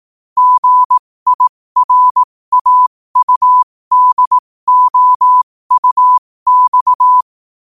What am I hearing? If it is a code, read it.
Morse code, heard as a keyed tone, about 18 wpm, spaced as for 15 wpm, 1000 Hz. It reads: GIRAUDOUX
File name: qotdfiles / 15wpm